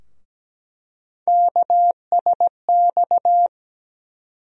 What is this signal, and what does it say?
Morse code, heard as a keyed tone, about 17 wpm, 700 Hz. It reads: KSX
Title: USA Call Sign Postfixes - 3 Letters 17wpm